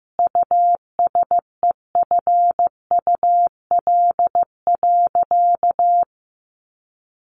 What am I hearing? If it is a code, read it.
Morse code, heard as a keyed tone, about 15 wpm, 700 Hz. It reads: USEFUL.
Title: Sentences from Top 700 Words 15wpm